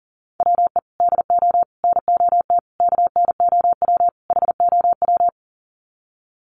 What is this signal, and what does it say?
Morse code, heard as a keyed tone, about 40 wpm, 700 Hz. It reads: WE DO NOT KNOW HOW
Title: Sets of 5 Words Encore 40wpm